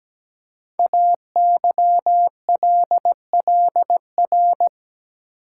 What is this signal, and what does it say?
Morse code, heard as a keyed tone, about 17 wpm, 700 Hz. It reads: AYLLR
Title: Code Groups - 5 Letters - 17wpm